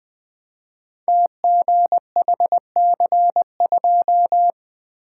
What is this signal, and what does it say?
Morse code, heard as a keyed tone, about 20 wpm, 700 Hz. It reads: TGHC2
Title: Code Groups - 5 Letter:Numbers - 20wpm